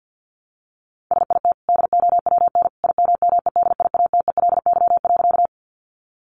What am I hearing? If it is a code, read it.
Morse code, heard as a keyed tone, about 50 wpm, 700 Hz. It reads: SIT DOWN IMMEDIATELY.
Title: Sentences from Top 1100 Words 50wpm